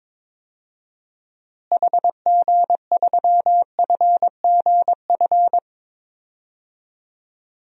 Morse code, heard as a keyed tone, about 21 wpm, 700 Hz. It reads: HG3FGF
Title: DXCC-100 Call Signs - 6 Characters - Repeated 3x - 22wpm